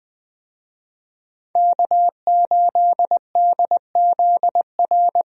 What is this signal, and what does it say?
Morse code, heard as a keyed tone, about 20 wpm, 700 Hz. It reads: K8DZR
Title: USA Call Signs - 5 Characters - Repeated 2x - 20wpm